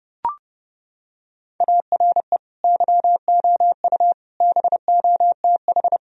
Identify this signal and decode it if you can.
Morse code, heard as a keyed tone, about 30 wpm, 700 Hz. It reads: ARE YOU BOTH
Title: Sentences from Top 1000 Words - Review Part 1 - 30wpm